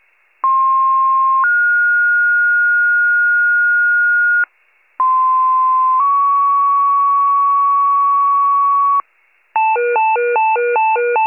Two-tone page, heard as a one-second tone, then a three-second tone, about 1060 then 1430 Hz, about 1040 then 1120 Hz. It is heard from 0.4 to 9.0 s.
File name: wcalert